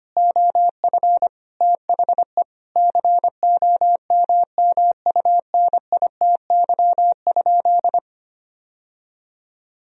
Morse code, heard as a keyed tone, about 25 wpm, 700 Hz. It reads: OF THE COMMUNITY?
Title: Sentences from Top 1000 Words - Review Part 4 - 25wpm